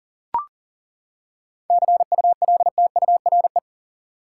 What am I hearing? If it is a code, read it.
Morse code, heard as a keyed tone, about 40 wpm, 700 Hz. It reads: CULTURE
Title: Seven Letter Words - 40wpm